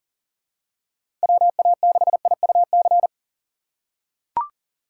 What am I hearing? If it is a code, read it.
Morse code, heard as a keyed tone, about 40 wpm, 700 Hz. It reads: WA6IUC
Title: USA Call Signs - 6 Characters - 4x Character Spacing - 40wpm